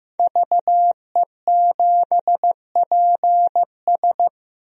Morse code, heard as a keyed tone, about 15 wpm, 700 Hz. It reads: VE7PS
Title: DXCC-100 Call Signs - 5 Characters - 15wpm